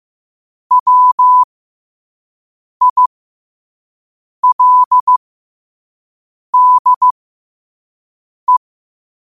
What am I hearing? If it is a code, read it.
Morse code, heard as a keyed tone, about 15 wpm, spaced as for 5 wpm, 1000 Hz. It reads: WILDE
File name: qotdfiles / 5wpm